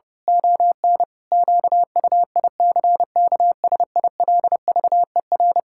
Morse code, heard as a keyed tone, about 30 wpm, 700 Hz. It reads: ON QUICKSILVER
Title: Pangram Challenge v2 - 30wpm